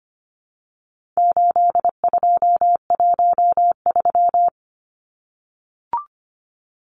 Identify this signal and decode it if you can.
Morse code, heard as a keyed tone, about 25 wpm, 700 Hz. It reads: 8213
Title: Numbers - 4-Digit 25wpm